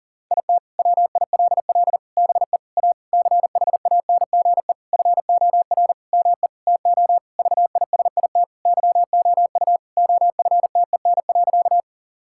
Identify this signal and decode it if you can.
Morse code, heard as a keyed tone, about 40 wpm, 700 Hz. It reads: IT WILL BE A CHANGE FOR ME TO VISIT YOU OFTEN.
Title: Sentences from Top 1000 Words - Review Part 6 - 40wpm